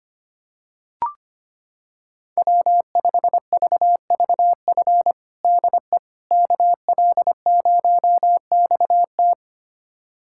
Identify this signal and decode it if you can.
Morse code, heard as a keyed tone, about 25 wpm, 700 Hz. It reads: W5VVF DE KL0XT
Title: USA Callsign DE Callsign - 5 Characters - 25wpm